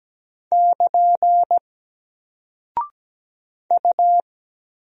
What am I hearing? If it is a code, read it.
Morse code, heard as a keyed tone, about 17 wpm, 700 Hz. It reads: <KN> U
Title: Single Character (Letter:Number:Punctuation:Prosign) Alt 17wpm